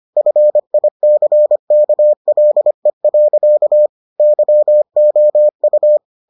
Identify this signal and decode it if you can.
Morse code, heard as a keyed tone, about 25 wpm, 600 Hz. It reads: FICKLE. YOU